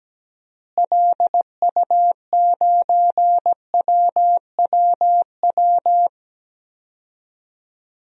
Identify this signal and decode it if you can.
Morse code, heard as a keyed tone, about 17 wpm, 700 Hz. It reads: LU9WWW
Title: DXCC-100 Call Signs - 6 Characters - Repeated 2x - 17wpm